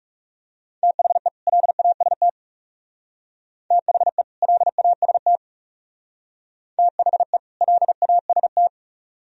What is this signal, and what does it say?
Morse code, heard as a keyed tone, about 34 wpm, 700 Hz. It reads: THE LAST THE LAST THE LAST
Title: Speed-Racing - Sets of 2 Words 30wpm